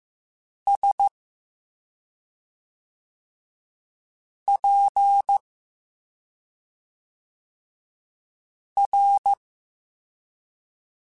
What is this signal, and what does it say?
Morse code, heard as a keyed tone, about 15 wpm, spaced as for 2 wpm, 800 Hz. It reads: SPR